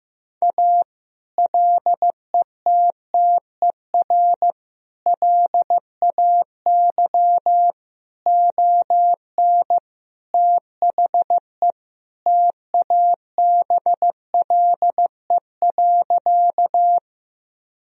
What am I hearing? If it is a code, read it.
Morse code, heard as a keyed tone, about 15 wpm, 700 Hz. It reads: A LETTER LAY ON THE TABLE.
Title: Sentences from Top 1000 Words - Review Part 2 - 15wpm